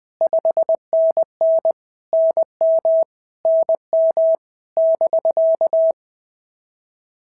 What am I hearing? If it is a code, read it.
Morse code, heard as a keyed tone, about 20 wpm, 650 Hz. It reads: 5NN NM NM <BK>
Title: POTA QSO Practice - 20wpm